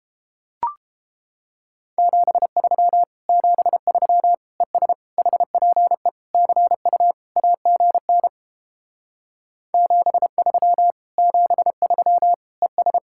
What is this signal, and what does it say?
Morse code, heard as a keyed tone, about 29 wpm, 700 Hz. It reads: 73 73 ES HPE CU AGN 73 73 ES